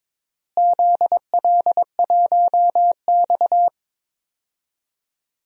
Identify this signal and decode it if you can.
Morse code, heard as a keyed tone, about 22 wpm, 700 Hz. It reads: ZL1X
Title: DXCC-100 Call Signs - 4 Characters - Repeated 3x - 22wpm